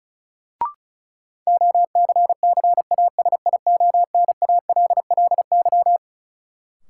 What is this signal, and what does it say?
Morse code, heard as a keyed tone, about 36 wpm, 700 Hz. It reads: OCCASIONALLY
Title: Sentences from Top 1300 Words - New Word Drill - 35wpm